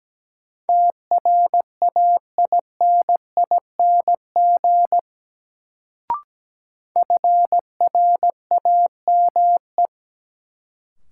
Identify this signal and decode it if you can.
Morse code, heard as a keyed tone, about 17 wpm, 700 Hz. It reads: TRAINING FRAME
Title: Sentences from Top 1400 Words - New Word Drill Alt - 17wpm